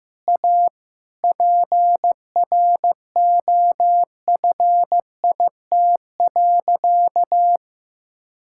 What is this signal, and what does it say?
Morse code, heard as a keyed tone, about 15 wpm, 700 Hz. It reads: A PROFIT.
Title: Sentences from Top 1000 Words - Review Part 5 - 15wpm